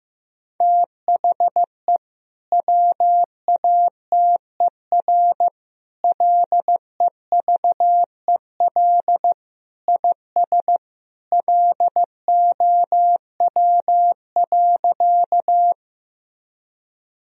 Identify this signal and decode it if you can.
Morse code, heard as a keyed tone, about 16 wpm, 700 Hz. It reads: THE WATER LEVEL IS LOW.
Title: Sentences from Top 1000 Words - Review Part 4 - 15wpm